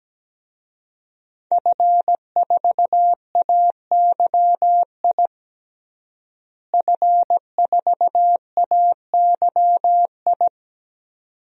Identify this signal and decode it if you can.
Morse code, heard as a keyed tone, about 17 wpm, 700 Hz. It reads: F4AYI F4AYI
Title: DXCC-100 Call Signs - 5 Characters - Repeated 2x - 17wpm